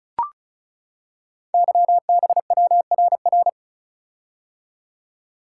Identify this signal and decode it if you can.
Morse code, heard as a keyed tone, about 36 wpm, 690 Hz. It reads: YBWRR
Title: Code Groups - 5 Letters - 35wpm